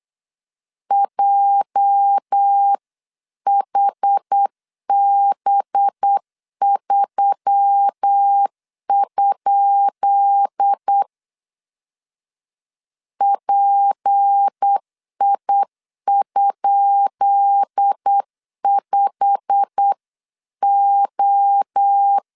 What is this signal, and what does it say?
Morse code, heard as a keyed tone, about 8 wpm, 800 Hz. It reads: JHB3? PI?5O
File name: LEZIONI / 10 SIGNS / 10B INT 8wpm 3534